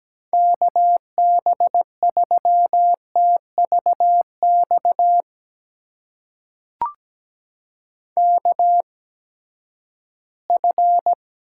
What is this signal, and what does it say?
Morse code, heard as a keyed tone, about 17 wpm, spaced as for 12 wpm, 700 Hz. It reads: KB3TVX KF
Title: USA Call Signs - 6 Characters - 8x Character Spacing - 17wpm